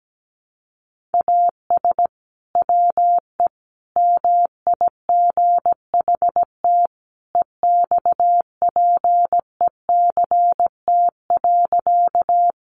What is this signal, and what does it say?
Morse code, heard as a keyed tone, about 17 wpm, 700 Hz. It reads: AS WE MIGHT EXPECT.
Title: Sentences from Top 1100 Words 17wpm